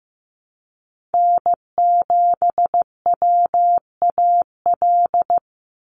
Morse code, heard as a keyed tone, about 15 wpm, 700 Hz. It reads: N7WAL
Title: USA Call Signs 15wpm